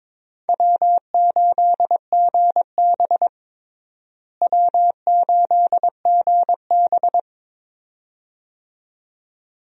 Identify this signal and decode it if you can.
Morse code, heard as a keyed tone, about 22 wpm, 700 Hz. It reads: W8GB W8GB
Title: USA Call Signs - 4 Characters - Repeated 2x - 22wpm